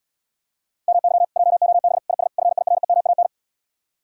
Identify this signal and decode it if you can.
Morse code, heard as a keyed tone, about 80 wpm, 700 Hz. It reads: MY JOB IS FIREMAN